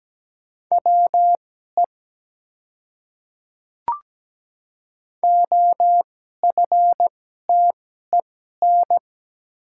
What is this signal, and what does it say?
Morse code, heard as a keyed tone, about 17 wpm, spaced as for 12 wpm, 700 Hz. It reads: WE OFTEN
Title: 2x Character Spacing - Top 300 Words - 17wpm